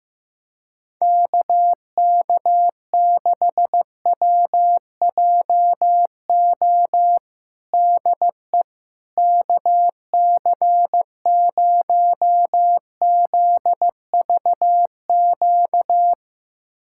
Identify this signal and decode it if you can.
Morse code, heard as a keyed tone, about 15 wpm, 700 Hz. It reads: KK6WJO DE KC0ZVQ
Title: USA Callsign DE Callsign 15wpm